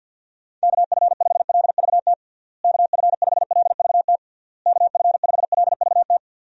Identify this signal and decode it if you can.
Morse code, heard as a keyed tone, about 52 wpm, 700 Hz. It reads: KF5LVT KF5LVT KF5LVT
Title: USA Call Signs - 6 Characters - Repeated 3x - 50wpm